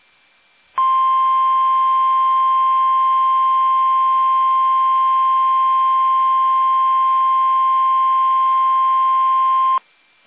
Emergency Alert System (SAME) tones (the 1050 Hz attention tone). It is heard from 0.8 to 9.8 s.
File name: eas / KUSO / KUSO